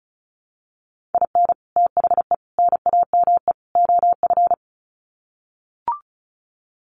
Morse code, heard as a keyed tone, about 34 wpm, 700 Hz. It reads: IN THE NAME OF
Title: Sets of 4 Words 35wpm